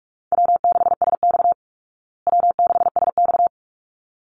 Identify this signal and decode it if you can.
Morse code, heard as a keyed tone, about 44 wpm, 700 Hz. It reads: W6SX W6SX
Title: Call Signs Repeated 3x - 45wpm